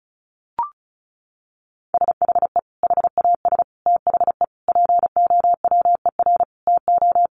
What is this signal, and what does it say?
Morse code, heard as a keyed tone, about 34 wpm, 700 Hz. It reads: SHE HAS THE POWER TO
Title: Sentences from Top 1200 Words 35wpm